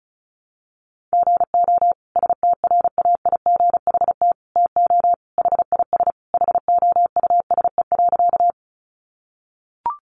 Morse code, heard as a keyed tone, about 34 wpm, 700 Hz. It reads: GO STRAIGHT TO HIS HOUSE.
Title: Sentences from Top 1000 Words 35wpm